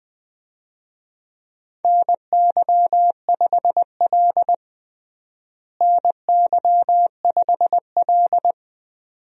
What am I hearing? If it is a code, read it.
Morse code, heard as a keyed tone, about 21 wpm, 700 Hz. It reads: NY5L NY5L
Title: USA Call Signs - 4 Characters - Repeated 2x - 20wpm